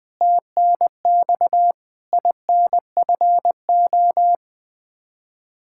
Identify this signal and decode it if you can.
Morse code, heard as a keyed tone, about 20 wpm, 700 Hz. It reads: TNX INFO
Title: Practice QSO - Exchange 3 - More Info - 20wpm